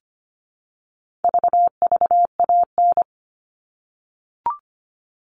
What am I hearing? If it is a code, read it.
Morse code, heard as a keyed tone, about 25 wpm, 700 Hz. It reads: VVAN
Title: All Cut Numbers - 4-Digit 25wpm